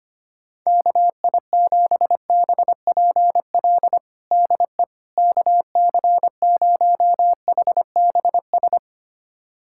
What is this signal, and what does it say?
Morse code, heard as a keyed tone, about 25 wpm, 700 Hz. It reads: KI7BPL DE KC0HBS
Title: USA Callsign DE Callsign - 6 Characters - 25wpm